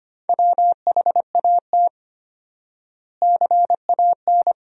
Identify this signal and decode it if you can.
Morse code, heard as a keyed tone, about 25 wpm, 700 Hz. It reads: WHAT CAN
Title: 4x Word-Spacing - Sentences from Top 100 Words - 25wpm